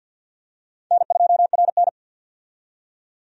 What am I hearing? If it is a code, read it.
Morse code, heard as a keyed tone, about 51 wpm, 700 Hz. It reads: N2RN
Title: USA Call Signs - 4 Characters - 50wpm